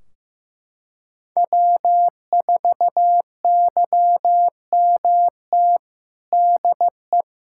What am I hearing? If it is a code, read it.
Morse code, heard as a keyed tone, about 15 wpm, 700 Hz. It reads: W4YMT DE
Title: Practice QSO - Exchange 4 - Ending - 15wpm